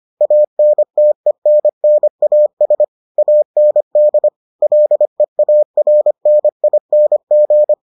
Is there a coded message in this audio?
Yes: Morse code from 0.2 to 7.7 s, heard as a keyed tone, about 25 wpm, 600 Hz. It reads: ANTENNAS AND LEARNING